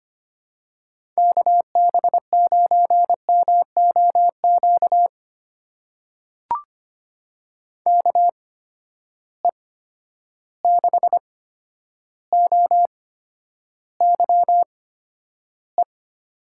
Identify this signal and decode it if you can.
Morse code, heard as a keyed tone, about 26 wpm, spaced as for 11 wpm, 700 Hz. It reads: KB9MOQ KE6OYE
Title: USA Call Signs - 6 Characters - 8x Character Spacing - 25wpm